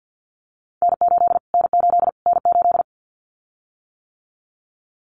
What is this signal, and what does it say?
Morse code, heard as a keyed tone, about 50 wpm, 700 Hz. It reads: N8 N8 N8
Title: Callsign Prefixes for Top 25 DXCC - Repeated 3x - 50wpm